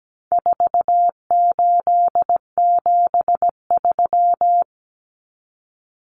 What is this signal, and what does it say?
Morse code, heard as a keyed tone, about 17 wpm, 700 Hz. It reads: 4873
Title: Common Cut Numbers - 4-Digit 17wpm